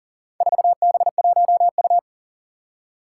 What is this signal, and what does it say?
Morse code, heard as a keyed tone, about 39 wpm, 700 Hz. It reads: 4B1U
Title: Four Challenging Letter-Number Combinations 40wpm